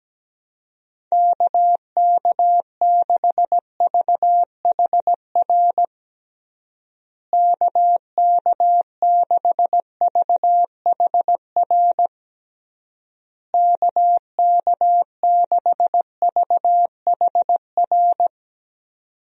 Morse code, heard as a keyed tone, about 17 wpm, 700 Hz. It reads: KK6VHR KK6VHR KK6VHR
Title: USA Call Signs - 6 Characters - Repeated 3x - 17wpm